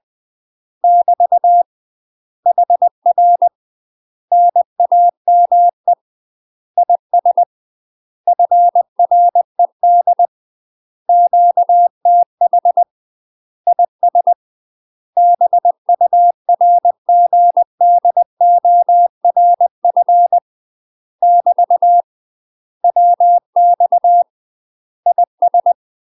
Morse code, heard as a keyed tone, about 20 wpm, 700 Hz. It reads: <BT> HR NAME IS FRED QTH IS BURGDORF <BT> WX IS
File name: UK MASTER QSOs TRAINING / 20WPM / UK MASTER QSOs 0047